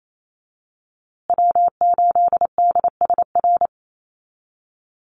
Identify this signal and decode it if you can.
Morse code, heard as a keyed tone, about 28 wpm, 700 Hz. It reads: W8DSR
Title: USA Call Signs Repeated 3x 28wpm